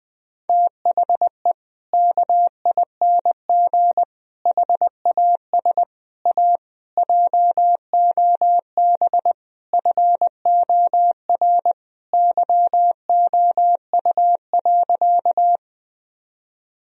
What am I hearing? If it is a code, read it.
Morse code, heard as a keyed tone, about 21 wpm, 700 Hz. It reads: THE KING HAS A JOB FOR YOU.
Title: Sentences from Top 1000 Words - Review Part 3 - 20wpm